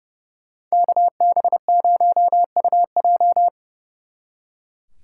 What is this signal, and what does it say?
Morse code, heard as a keyed tone, about 37 wpm, 700 Hz. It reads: KB0UJ
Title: Speed Racing - USA Call Signs 30wpm